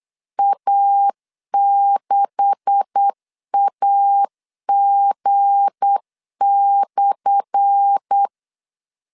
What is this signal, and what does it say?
Morse code, heard as a keyed tone, about 8 wpm, 800 Hz. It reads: A6AG/
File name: LEZIONI / 10 SIGNS / 10A SLASH 8wpm 3452